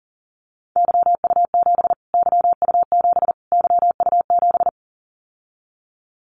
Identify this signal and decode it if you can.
Morse code, heard as a keyed tone, about 40 wpm, 700 Hz. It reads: YU7 YU7 YU7
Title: Callsign Prefixes for Top 25 DXCC - Repeated 3x - 40wpm